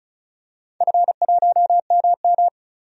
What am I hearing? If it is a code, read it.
Morse code, heard as a keyed tone, about 34 wpm, 700 Hz. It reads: F1MM
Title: DXCC-100 Call Signs - 4 Characters - 35wpm